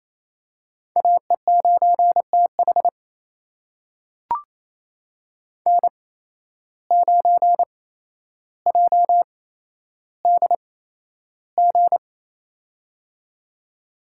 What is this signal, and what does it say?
Morse code, heard as a keyed tone, about 28 wpm, spaced as for 12 wpm, 700 Hz. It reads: AE9TH N9JDG
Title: USA Call Signs - 5 Characters - 8x Character Spacing - 28wpm